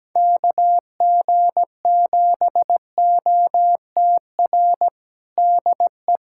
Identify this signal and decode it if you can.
Morse code, heard as a keyed tone, about 17 wpm, 700 Hz. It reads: KG7OTR DE
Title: USA Callsign DE Callsign - 6 Characters - 17wpm